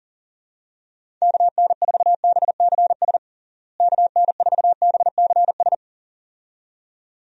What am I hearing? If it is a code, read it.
Morse code, heard as a keyed tone, about 40 wpm, 700 Hz. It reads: KN4BCS KN4BCS
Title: USA Call Signs - 6 Characters - Repeated 2x - 40wpm